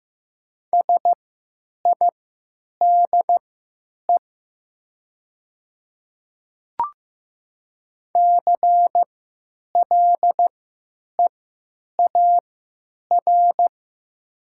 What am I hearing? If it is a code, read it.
Morse code, heard as a keyed tone, about 15 wpm, spaced as for 9 wpm, 700 Hz. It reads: SIDE CLEAR
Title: 3x Character Spacing - Top 500 Words - 15wpm